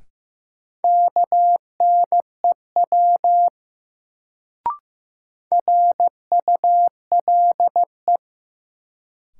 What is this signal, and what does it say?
Morse code, heard as a keyed tone, about 15 wpm, 700 Hz. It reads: KNEW RULE
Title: Sentences from Top 400 Words - New Word Drill - 15wpm